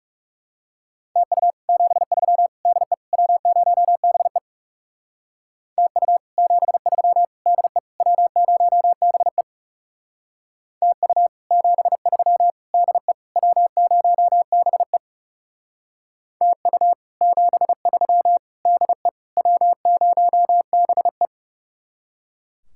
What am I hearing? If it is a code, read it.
Morse code, heard as a keyed tone, about 37 wpm, 700 Hz. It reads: TU 73 DE W0BE TU 73 DE W0BE TU 73 DE W0BE TU 73 DE W0BE
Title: Speed-Racing Common QSO Phrases - 30wpm